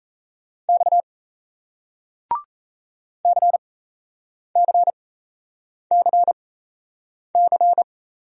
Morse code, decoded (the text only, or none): X CCCC